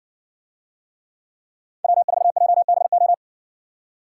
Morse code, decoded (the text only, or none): W4JBO